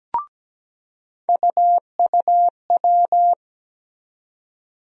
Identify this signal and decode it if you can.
Morse code, heard as a keyed tone, about 17 wpm, 700 Hz. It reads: UUW
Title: USA Call Sign Postfixes - 3 Letters 17wpm